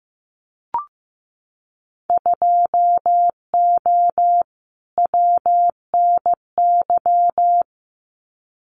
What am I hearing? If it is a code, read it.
Morse code, heard as a keyed tone, about 15 wpm, 700 Hz. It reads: 2O WNY
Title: Winter Field Day Exchanges Encore 15wpm